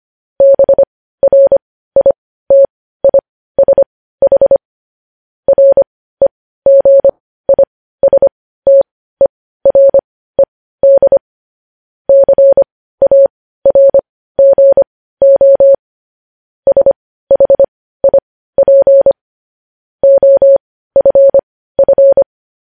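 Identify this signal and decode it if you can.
Morse code, heard as a keyed tone, about 25 wpm, spaced as for 15 wpm, 550 Hz. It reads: BRITISH REGISTERED CARGO SHIP OFF
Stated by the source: Morse code news headlines.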